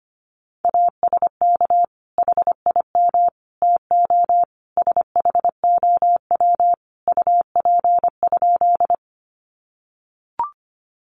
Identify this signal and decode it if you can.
Morse code, heard as a keyed tone, about 25 wpm, 700 Hz. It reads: ASK HIM TO SHOW UP?
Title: Sentences from Top 200 Words 25wpm